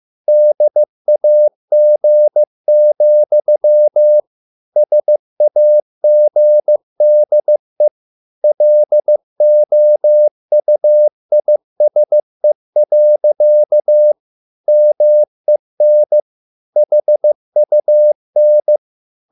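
Morse code, heard as a keyed tone, about 15 wpm, 600 Hz. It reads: DAG, SAGDE LOUISE. MEN HUN
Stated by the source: Morse code at a slow 15 wpm from CW med Gnister.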